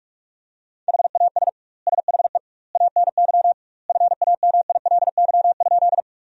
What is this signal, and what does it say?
Morse code, heard as a keyed tone, about 41 wpm, 700 Hz. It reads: HAS SHE ANY FAMILY?
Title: 2x Word-Spacing - Sentences from Top 400 Words - 45wpm